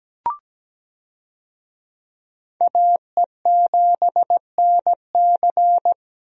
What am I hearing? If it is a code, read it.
Morse code, heard as a keyed tone, about 17 wpm, 700 Hz. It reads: AE7NC
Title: USA Call Signs - 5 Characters - Repeated 2x - 17wpm